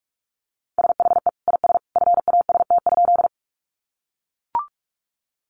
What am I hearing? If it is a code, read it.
Morse code, heard as a keyed tone, about 44 wpm, 700 Hz. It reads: SHE IS FAST?
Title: Sentences from Top 400 Words 45wpm